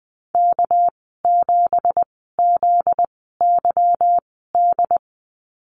Morse code, heard as a keyed tone, about 20 wpm, spaced as for 14 wpm, 700 Hz. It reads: K7ZYD
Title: 245 Lesson 39g - Letter z - Call Signs 20wpm